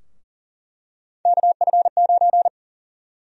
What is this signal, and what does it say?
Morse code, heard as a keyed tone, about 39 wpm, 700 Hz. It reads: KF9
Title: USA Call Sign Prefixes - 40wpm